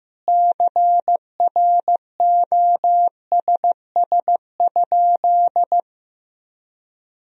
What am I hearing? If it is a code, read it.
Morse code, heard as a keyed tone, about 16 wpm, 700 Hz. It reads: CROSS?
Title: Sentences from Top 1000 Words - Review Part 1 - 15wpm